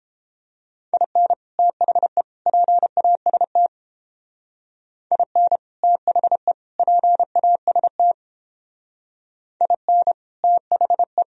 Morse code, heard as a keyed tone, about 29 wpm, 700 Hz. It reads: IN THE PAST IN THE PAST IN THE